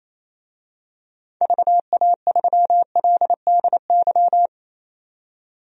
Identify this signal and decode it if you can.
Morse code, heard as a keyed tone, about 28 wpm, 700 Hz. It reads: VA3LDY
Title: DXCC-100 Call Signs - 6 Characters - 28wpm